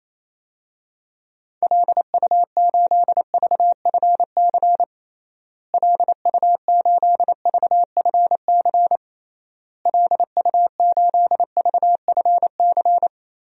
Morse code, heard as a keyed tone, about 28 wpm, 700 Hz. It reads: LU8VFC LU8VFC LU8VFC
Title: DXCC-100 Call Signs - 6 Characters - Repeated 3x - 28wpm